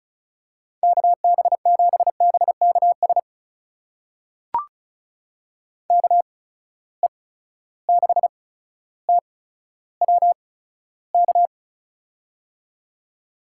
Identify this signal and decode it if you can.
Morse code, heard as a keyed tone, about 34 wpm, spaced as for 15 wpm, 700 Hz. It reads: KB7BKS KE6TWK
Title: USA Call Signs - 8x Character Spacing - 35wpm